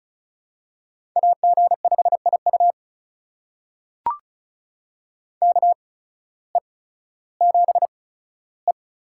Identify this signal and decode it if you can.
Morse code, heard as a keyed tone, about 36 wpm, spaced as for 16 wpm, 700 Hz. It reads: AG5IU KE7E
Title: USA Call Signs - 8x Character Spacing - 35wpm